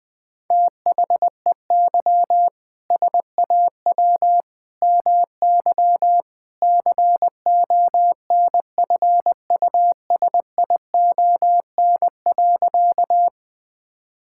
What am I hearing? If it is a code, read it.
Morse code, heard as a keyed tone, about 20 wpm, 700 Hz. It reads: THEY SAW MY CONFUSION.
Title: Sentences from Top 1600 Words 20wpm